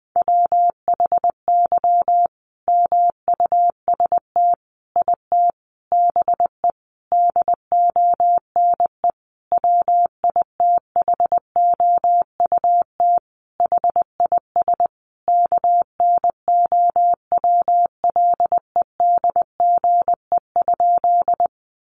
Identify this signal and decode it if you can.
Morse code, heard as a keyed tone, about 19 wpm, 700 Hz. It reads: WHY MUST IT BE DONE WITHOUT HIS KNOWLEDGE?
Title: Sentences from Top 600 Words 20wpm